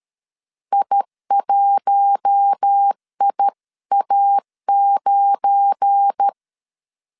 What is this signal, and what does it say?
Morse code, heard as a keyed tone, about 13 wpm, 800 Hz. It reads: I1IA9
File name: CW / DET / 1 9 1500C S B 12wpm 3819